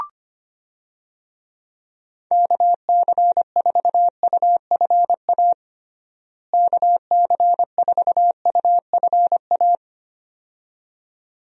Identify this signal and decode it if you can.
Morse code, heard as a keyed tone, about 25 wpm, 700 Hz. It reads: KC4UFA KC4UFA
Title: USA Call Signs - 6 Characters - Repeated 2x - 25wpm